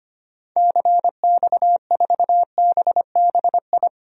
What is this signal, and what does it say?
Morse code, heard as a keyed tone, about 25 wpm, 700 Hz. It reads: CX4BBI
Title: DXCC-100 Call Signs - 6 Characters - Repeated 2x - 25wpm